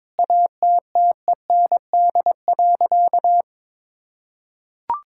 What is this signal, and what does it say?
Morse code, heard as a keyed tone, about 22 wpm, 700 Hz. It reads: ATTEND.
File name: Sentences with 7-Word Phrases - 22wpm